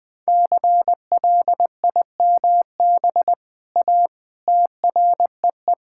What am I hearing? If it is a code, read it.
Morse code, heard as a keyed tone, about 20 wpm, 700 Hz. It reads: CLIMB A TREE
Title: All-Age Appropriate Jokes - 20wpm